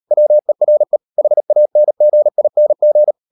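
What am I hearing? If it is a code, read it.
Morse code, heard as a keyed tone, about 38 wpm, 600 Hz. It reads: WERE HANGING
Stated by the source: Morse code practice at 38 WPM, sending Ken Follett's novel Never in English, from CW med Gnister.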